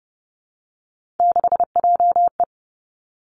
Morse code, heard as a keyed tone, about 30 wpm, 700 Hz. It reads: 6JE
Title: Three Challenging Letter-Number Combinations 30wpm